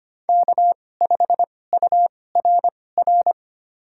Morse code, heard as a keyed tone, about 25 wpm, spaced as for 18 wpm, 700 Hz. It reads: K5URR
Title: USA Call Signs - 5 Characters - 2x Character Spacing - 25wpm